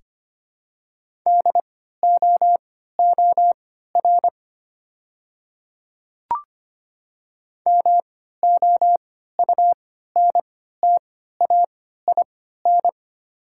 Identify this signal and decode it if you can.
Morse code, heard as a keyed tone, about 25 wpm, spaced as for 14 wpm, 690 Hz. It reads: DOOR MOUNTAIN
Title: 3x Character Spacing - Top 300 Words - 25wpm